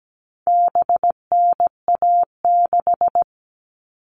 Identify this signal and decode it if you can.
Morse code, heard as a keyed tone, about 17 wpm, 700 Hz. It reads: BNA6
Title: All Cut Numbers - 4-Digit 17wpm